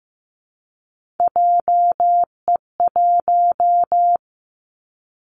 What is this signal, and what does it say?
Morse code, heard as a keyed tone, about 15 wpm, 700 Hz. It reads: JE1
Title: Callsign Prefixes for Top 25 DXCC - 15wpm